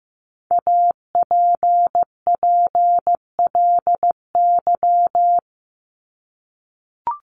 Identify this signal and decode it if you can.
Morse code, heard as a keyed tone, about 15 wpm, 700 Hz. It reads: APPLY
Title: ED Suffix - Top 100 Words with Base Words 15wpm